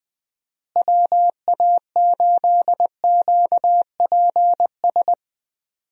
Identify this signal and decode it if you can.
Morse code, heard as a keyed tone, about 21 wpm, 700 Hz. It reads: WA8QPS
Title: USA Call Signs - 6 Characters - Repeated 3x - 20wpm